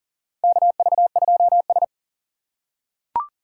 Morse code, heard as a keyed tone, about 40 wpm, 700 Hz. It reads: KV2S
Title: USA Call Signs - 4 Characters - 6x Character Spacing - 40wpm